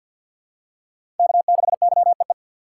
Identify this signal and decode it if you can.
Morse code, heard as a keyed tone, about 47 wpm, 700 Hz. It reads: K6YEE
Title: USA Call Signs - 5 Characters - 2x Character Spacing - 50wpm